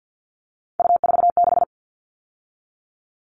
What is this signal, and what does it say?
Morse code, heard as a keyed tone, about 50 wpm, 700 Hz. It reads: U46
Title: All Cut Numbers - 3-Digit 50wpm